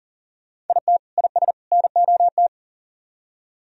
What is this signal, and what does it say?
Morse code, heard as a keyed tone, about 51 wpm, 700 Hz. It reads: IT IS NOT